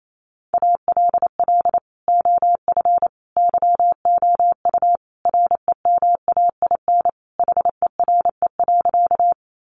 Morse code, heard as a keyed tone, about 28 wpm, 700 Hz. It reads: ALL OF YOU REMAIN HERE.
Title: Sentences from Top 1200 Words 28wpm